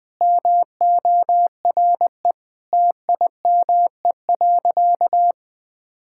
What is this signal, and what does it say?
Morse code, heard as a keyed tone, about 20 wpm, 700 Hz. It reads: MORE TIME.
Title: Sentences with 3-Word Phrases - 20wpm